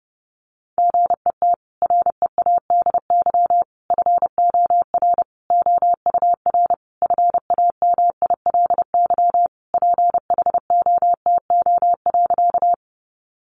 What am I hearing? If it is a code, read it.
Morse code, heard as a keyed tone, about 30 wpm, 700 Hz. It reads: GET READY FOR OUR FAMILY PHOTO.
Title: Sentences from Top 1200 Words 30wpm